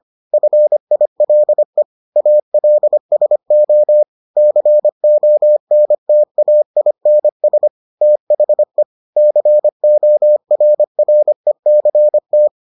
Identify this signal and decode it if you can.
Morse code, heard as a keyed tone, about 25 wpm, 600 Hz. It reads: FILE ALSO CONTAINS THE CORRECT